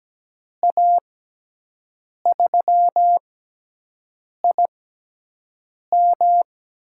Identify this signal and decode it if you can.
Morse code, heard as a keyed tone, about 17 wpm, spaced as for 6 wpm, 700 Hz. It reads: A3IM
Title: USA Call Signs - 6x Character Spacing - 17wpm